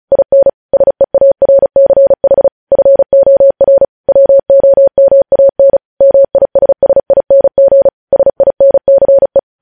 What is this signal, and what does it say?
Morse code, heard as a keyed tone, about 36 wpm, 550 Hz. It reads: IN SEARCH FOR WOMAN MISSING SINCE
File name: podcast / News headlines-1x-35x35